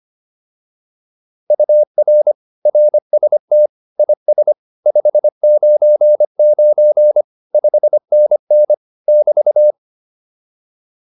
Morse code, heard as a keyed tone, about 25 wpm, 600 Hz. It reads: UR RST IS 599 5NN <BT>